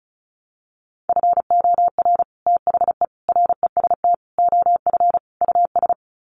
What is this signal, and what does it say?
Morse code, heard as a keyed tone, about 34 wpm, 700 Hz. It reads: FOR THE REST OF US
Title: Sets of 5 Words Encore 35wpm